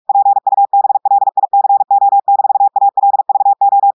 Morse code, heard as a keyed tone, about 46 wpm, 800 Hz. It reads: PUBLICO-ALVO
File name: audios / portugais / portugais texte en clair 2 vitesse 45 mots minutes